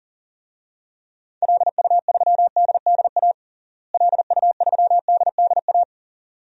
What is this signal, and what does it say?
Morse code, heard as a keyed tone, about 40 wpm, 700 Hz. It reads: LU3DDA LU3DDA
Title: DXCC-100 Call Signs - 6 Characters - Repeated 2x - 40wpm